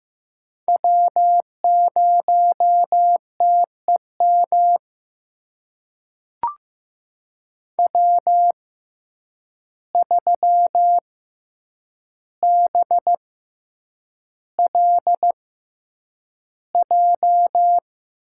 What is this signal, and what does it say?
Morse code, heard as a keyed tone, about 15 wpm, spaced as for 8 wpm, 700 Hz. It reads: W0TEM W3BLJ
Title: USA Call Signs - 5 Characters - 6x Character Spacing - 15wpm